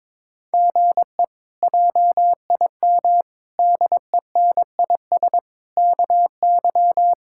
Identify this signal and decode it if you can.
Morse code, heard as a keyed tone, about 22 wpm, 700 Hz. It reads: GE JIM DENIS KY
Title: K1USN SST QSO Practice - 22wpm